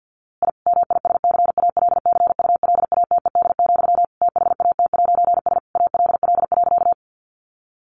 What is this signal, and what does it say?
Morse code, heard as a keyed tone, about 50 wpm, 700 Hz. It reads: I MISCALCULATED, THAT'S ALL.
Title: A Sound Of Thunder 50wpm